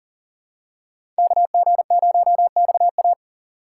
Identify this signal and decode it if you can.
Morse code, heard as a keyed tone, about 40 wpm, 700 Hz. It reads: KG0XA
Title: USA Call Signs - 6x Character Spacing - 40wpm